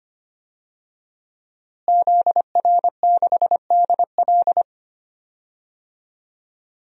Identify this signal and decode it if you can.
Morse code, heard as a keyed tone, about 25 wpm, 700 Hz. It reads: ZR6DL
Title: DXCC-100 Call Signs - 5 Characters - Repeated 3x - 25wpm